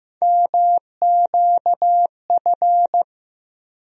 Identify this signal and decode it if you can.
Morse code, heard as a keyed tone, about 15 wpm, 700 Hz. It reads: MQF
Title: USA Call Sign Postfixes - 3 Letters 15wpm